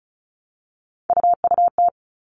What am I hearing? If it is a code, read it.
Morse code, heard as a keyed tone, about 34 wpm, 700 Hz. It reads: UUT
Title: All Cut Numbers - 3-Digit 35wpm